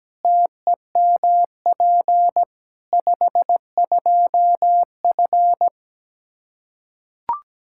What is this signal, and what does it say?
Morse code, heard as a keyed tone, about 17 wpm, 700 Hz. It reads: TEMP 52F
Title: Practice QSO - Exchange 2 - Info - 17wpm